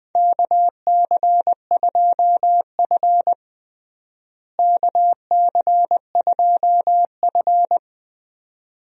Morse code, heard as a keyed tone, about 20 wpm, 700 Hz. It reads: KC2F KC2F
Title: USA Call Signs - 4 Characters - Repeated 2x - 20wpm